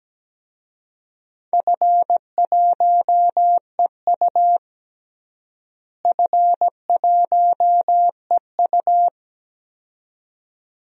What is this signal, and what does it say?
Morse code, heard as a keyed tone, about 17 wpm, 700 Hz. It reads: F1EU F1EU
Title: DXCC-100 Call Signs - 4 Characters - Repeated 2x - 17wpm